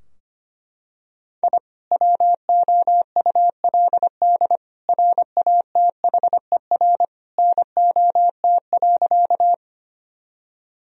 Morse code, heard as a keyed tone, about 25 wpm, 700 Hz. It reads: I WOULD RATHER NOT.
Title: Sentences from Top 1000 Words - Review Part 4 - 25wpm